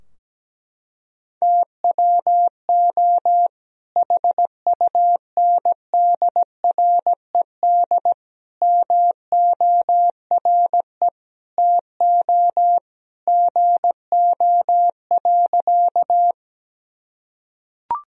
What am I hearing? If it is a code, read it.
Morse code, heard as a keyed tone, about 17 wpm, 700 Hz. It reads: TWO HUNDRED MORE TO GO.
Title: Sentences from Top 1000 Words - Review Part 2 - 17wpm